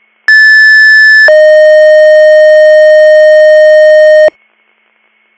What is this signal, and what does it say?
Two-tone page, heard as a one-second tone, then a three-second tone, about 1750 then 620 Hz. It is heard from 0.3 to 4.3 s.